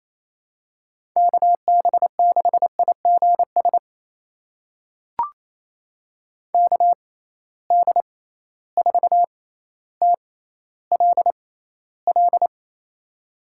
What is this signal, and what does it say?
Morse code, heard as a keyed tone, about 28 wpm, spaced as for 14 wpm, 700 Hz. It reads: KB6IGS KD4TLL
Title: USA Call Signs - 6 Characters - 6x Character Spacing - 28wpm